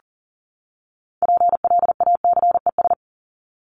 Morse code, heard as a keyed tone, about 40 wpm, 700 Hz. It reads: PLACES
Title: ES:S Suffix - Top 100 Words with Base Words 40wpm